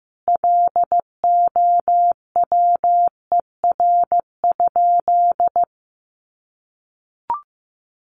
Morse code, heard as a keyed tone, about 15 wpm, 700 Hz. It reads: LOWER?
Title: Sentences from Top 700 Words 15wpm